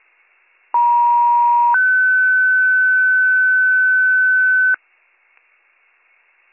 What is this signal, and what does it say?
Two-tone page, heard as a one-second tone, then a three-second tone, about 950 then 1500 Hz. It is heard from 0.7 to 4.8 s.